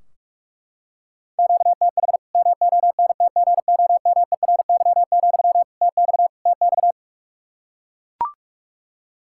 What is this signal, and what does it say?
Morse code, heard as a keyed tone, about 43 wpm, 700 Hz. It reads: QTH MONTGOMERY, TX TX
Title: K1USN SST QSO Practice 45wpm